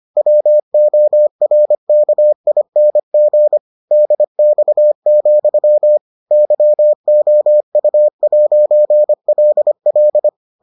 Morse code, heard as a keyed tone, about 25 wpm, 600 Hz. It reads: WORKING DX, YOU'LL